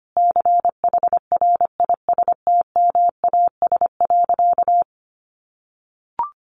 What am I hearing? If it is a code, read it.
Morse code, heard as a keyed tone, about 25 wpm, 700 Hz. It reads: CHRISTMAS.
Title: Easy Phrases 25wpm